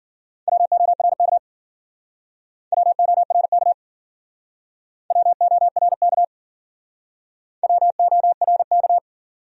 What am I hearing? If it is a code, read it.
Morse code, heard as a keyed tone, about 55 wpm, 700 Hz. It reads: WORK WORK WORK WORK